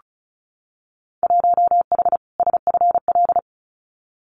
Morse code, heard as a keyed tone, about 34 wpm, 700 Hz. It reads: 1H SFL
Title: Winter Field Day Exchanges 35wpm